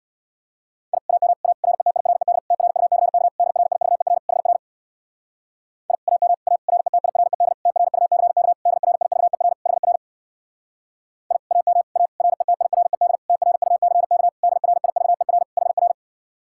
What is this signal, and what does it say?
Morse code, heard as a keyed tone, about 60 wpm, 700 Hz. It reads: I AM A RETIRED TRUCK DRIVER HR I AM A RETIRED TRUCK DRIVER HR I AM A RETIRED TRUCK DRIVER HR
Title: Speed-Racing Common QSO Phrases - 50wpm